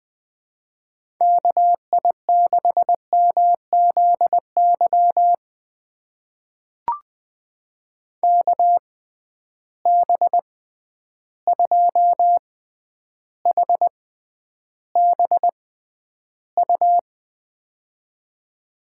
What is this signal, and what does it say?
Morse code, heard as a keyed tone, about 20 wpm, spaced as for 10 wpm, 700 Hz. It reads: KI6MZY KB2HBU
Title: USA Call Signs - 6 Characters - 6x Character Spacing - 20wpm